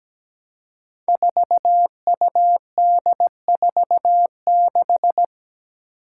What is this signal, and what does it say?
Morse code, heard as a keyed tone, about 17 wpm, 700 Hz. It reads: 4UD46
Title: Five Challenging Letter-Number Combinations 17wpm